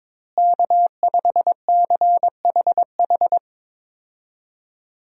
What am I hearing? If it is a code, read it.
Morse code, heard as a keyed tone, about 22 wpm, 700 Hz. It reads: K5CHH
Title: USA Call Signs - 5 Characters - Repeated 3x - 22wpm